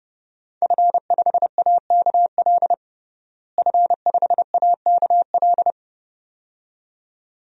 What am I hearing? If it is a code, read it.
Morse code, heard as a keyed tone, about 30 wpm, 700 Hz. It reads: F5AKL F5AKL
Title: DXCC-100 Call Signs - 5 Characters - Repeated 2x - 30wpm